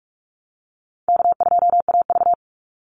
Morse code, heard as a keyed tone, about 44 wpm, 700 Hz. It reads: K2AV
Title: Call Signs - 45wpm